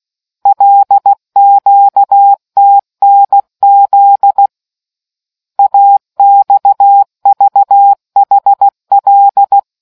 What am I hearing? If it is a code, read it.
Morse code, heard as a keyed tone, about 16 wpm, 800 Hz. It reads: LQTNZ AXVHL